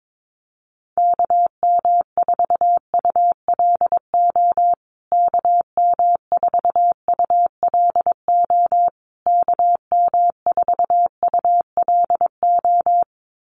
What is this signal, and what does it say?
Morse code, heard as a keyed tone, about 22 wpm, 700 Hz. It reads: KM4ULO KM4ULO KM4ULO
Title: USA Call Signs Repeated 3x 22wpm